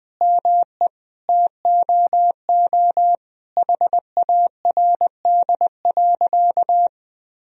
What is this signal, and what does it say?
Morse code, heard as a keyed tone, about 20 wpm, 700 Hz. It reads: ME TOO HARD.
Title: Sentences from Top 1000 Words - Review Part 1 - 20wpm